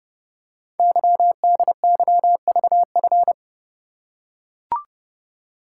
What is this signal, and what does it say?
Morse code, heard as a keyed tone, about 29 wpm, 700 Hz. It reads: YDYVF
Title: Code Groups - 5 Letters - 30wpm